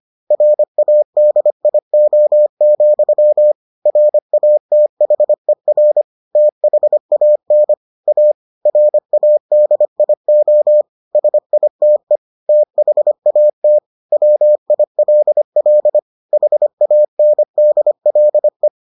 Morse code, heard as a keyed tone, about 25 wpm, 600 Hz. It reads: RADIO, RATHER THAN A RADIO SITE THAT WILL HANDLE